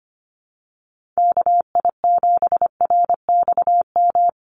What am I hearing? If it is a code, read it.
Morse code, heard as a keyed tone, about 25 wpm, 700 Hz. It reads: KI7RXM
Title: USA Call Signs 25wpm